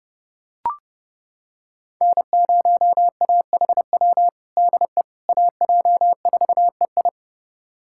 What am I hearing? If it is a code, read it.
Morse code, heard as a keyed tone, about 31 wpm, 700 Hz. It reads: N0AHW DE AJ4EI
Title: USA Callsign DE Callsign - 5 Characters - 30wpm